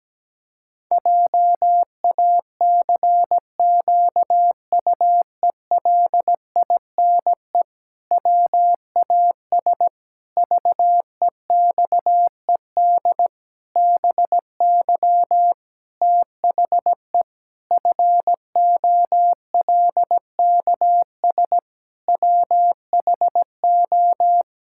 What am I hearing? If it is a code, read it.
Morse code, heard as a keyed tone, about 17 wpm, 700 Hz. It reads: JACQUELINE WAS VEXED BY THE FOLKS WHO
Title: Pangram Challenge v2 - 17wpm